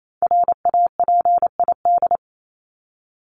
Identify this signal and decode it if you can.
Morse code, heard as a keyed tone, about 28 wpm, 700 Hz. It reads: RAPID
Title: LY Suffix - Top 100 Words with Base Words 28wpm